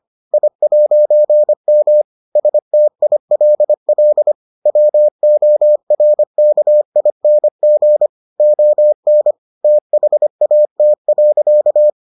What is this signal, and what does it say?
Morse code, heard as a keyed tone, about 25 wpm, 600 Hz. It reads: I'M STILL WORKING ON THAT.